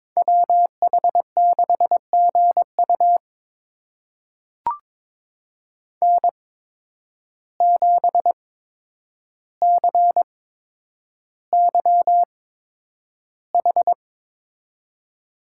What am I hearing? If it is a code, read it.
Morse code, heard as a keyed tone, about 23 wpm, spaced as for 10 wpm, 700 Hz. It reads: WH6GU N7CYH
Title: USA Call Signs - 5 Characters - 8x Character Spacing - 22wpm